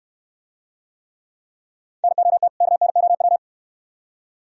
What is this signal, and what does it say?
Morse code, heard as a keyed tone, about 66 wpm, 700 Hz. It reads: NOT KNOW